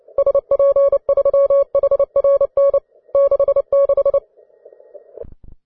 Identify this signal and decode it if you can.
Morse code, heard as a keyed tone, about 29 wpm, 560 Hz. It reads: SP3HRN 66
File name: wae / cw advanced / ve3kz cw